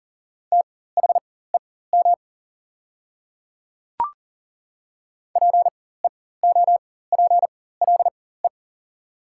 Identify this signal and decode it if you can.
Morse code, heard as a keyed tone, about 40 wpm, spaced as for 19 wpm, 700 Hz. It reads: THEM PEOPLE
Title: 4x Character Spacing - Top 300 Words - 40wpm